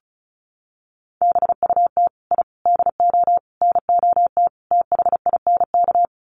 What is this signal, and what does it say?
Morse code, heard as a keyed tone, about 34 wpm, 700 Hz. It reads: BUT I DO NOT THINK
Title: Sets of 5 Words 35wpm